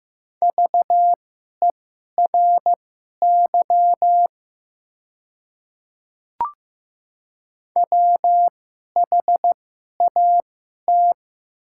Morse code, heard as a keyed tone, about 15 wpm, spaced as for 11 wpm, 700 Hz. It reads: VERY WHAT
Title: 2x Character Spacing - Top 200 Words - 15wpm